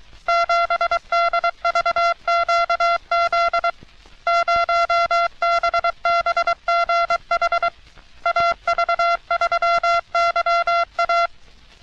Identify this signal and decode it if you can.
Morse code, heard as a keyed tone, about 22 wpm, 680 Hz. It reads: 7DVQZ 0BBGH AV3YA
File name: U.S. Navy - RN-2849-124 - Sixteens Morse Code Practice